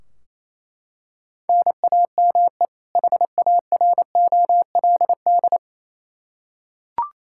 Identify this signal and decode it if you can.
Morse code, heard as a keyed tone, about 28 wpm, 700 Hz. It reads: NAME HAROLD
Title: Practice QSO - Exchange 1 - Beginning - 28wpm